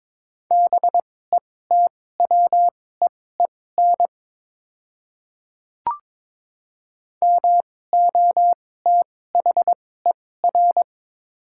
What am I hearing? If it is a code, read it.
Morse code, heard as a keyed tone, about 22 wpm, spaced as for 16 wpm, 690 Hz. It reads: BETWEEN MOTHER
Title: 2x Character Spacing - Top 500 Words - 22wpm